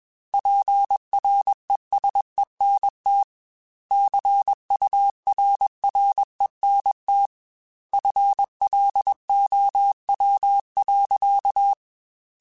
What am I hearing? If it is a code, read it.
Morse code, heard as a keyed tone, about 21 wpm, 800 Hz. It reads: PRESENT CURRENT FLOW.